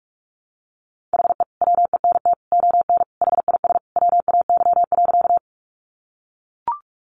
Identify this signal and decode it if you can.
Morse code, heard as a keyed tone, about 44 wpm, 700 Hz. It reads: HE WENT ON HIS WAY.
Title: Sentences from Top 200 Words 45wpm